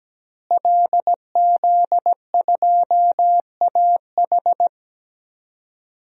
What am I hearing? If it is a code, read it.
Morse code, heard as a keyed tone, about 17 wpm, 700 Hz. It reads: LZ2AH
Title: DXCC-100 Call Signs - 5 Characters - Repeated 2x - 17wpm